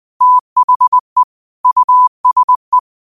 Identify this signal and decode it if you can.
Morse code, heard as a keyed tone, about 20 wpm, 1000 Hz. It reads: THE USE
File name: qotdfiles / 20wpm